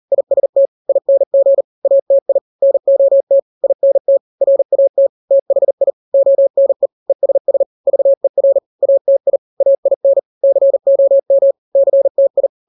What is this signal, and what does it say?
Morse code, heard as a keyed tone, about 40 wpm, 550 Hz. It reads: IST ING ATI NOT INT RAT THI ONE ESS VER ATI AIN COM CTI
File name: n-grams / 3-grams 40 35 WPM